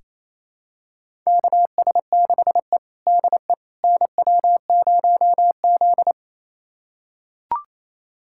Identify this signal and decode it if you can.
Morse code, heard as a keyed tone, about 27 wpm, 690 Hz. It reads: KS6E DE NW0Z
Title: USA Callsign DE Callsign - 4 Characters - 28wpm